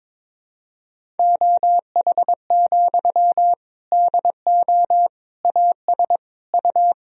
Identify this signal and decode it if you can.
Morse code, heard as a keyed tone, about 22 wpm, 700 Hz. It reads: OH, DO AS U
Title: Sentences from Top 500 Words and Abbreviations 22wpm